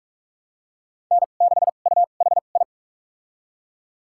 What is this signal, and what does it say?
Morse code, heard as a keyed tone, about 45 wpm, spaced as for 32 wpm, 700 Hz. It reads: N6UHI
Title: USA Call Signs - 5 Characters - 2x Character Spacing - 45wpm